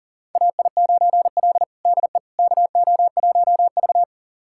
Morse code, heard as a keyed tone, about 40 wpm, 700 Hz. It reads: AI9L DE KO1V
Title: USA Callsign DE Callsign - 4 Characters - 40wpm